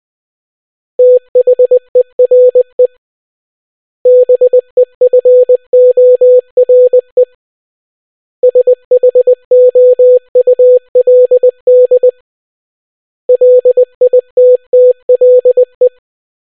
Morse code, heard as a keyed tone, about 20 wpm, 500 Hz. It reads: THERE BEFORE SHOULD LITTLE